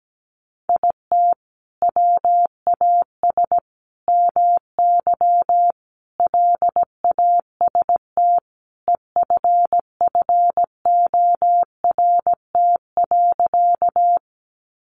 Morse code, heard as a keyed tone, about 17 wpm, 700 Hz. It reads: IT WAS MY LAST EFFORT.
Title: Sentences from Top 1200 Words 17wpm